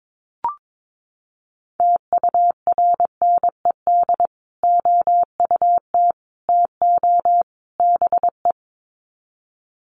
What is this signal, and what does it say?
Morse code, heard as a keyed tone, about 22 wpm, 700 Hz. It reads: TURNED OUT TO BE
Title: Sets of 4 Words 22wpm